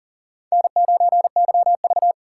Morse code, heard as a keyed tone, about 40 wpm, 700 Hz. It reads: N9YV
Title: USA Call Signs - 4 Characters - 6x Character Spacing - 40wpm